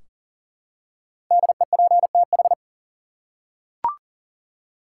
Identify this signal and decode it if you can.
Morse code, heard as a keyed tone, about 40 wpm, 700 Hz. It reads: DEPTH
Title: Five Letter Words - 40wpm